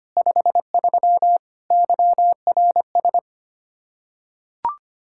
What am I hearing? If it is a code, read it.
Morse code, heard as a keyed tone, about 25 wpm, 700 Hz. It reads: 53 YRS
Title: Practice QSO - Exchange 3 - More Info - 25wpm